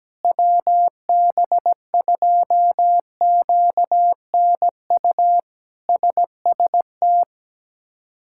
Morse code, heard as a keyed tone, about 17 wpm, 700 Hz. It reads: WB2QNU SST
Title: K1USN SST QSO Practice - 17wpm